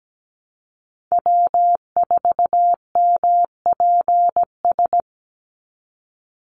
Morse code, heard as a keyed tone, about 17 wpm, 700 Hz. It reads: W4MPS
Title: NA SOTA Chaser Callsigns 17wpm